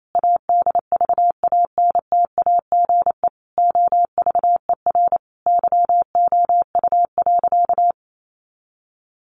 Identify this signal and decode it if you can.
Morse code, heard as a keyed tone, about 28 wpm, 700 Hz. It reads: ADVANTAGE OVER YOU.
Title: Sentences from Top 900 Words 28wpm